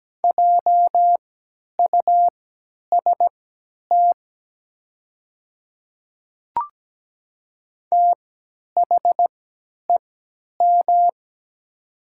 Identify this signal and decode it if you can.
Morse code, heard as a keyed tone, about 17 wpm, spaced as for 10 wpm, 700 Hz. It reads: JUST THEM
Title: 3x Character Spacing - Top 200 Words - 17wpm